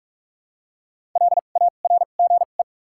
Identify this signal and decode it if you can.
Morse code, heard as a keyed tone, about 46 wpm, 700 Hz. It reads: L A R G E
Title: 2x Character Spacing - Top 300 Words - 45wpm